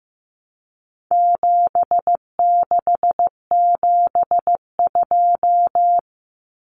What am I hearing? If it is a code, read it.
Morse code, heard as a keyed tone, about 15 wpm, 700 Hz. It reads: 7672
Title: Common Cut Numbers - 4-Digit 15wpm